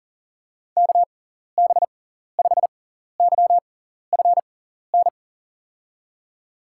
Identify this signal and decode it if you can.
Morse code, heard as a keyed tone, about 38 wpm, spaced as for 14 wpm, 700 Hz. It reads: KB5YFN
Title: USA Call Signs - 6 Characters - 6x Character Spacing - 40wpm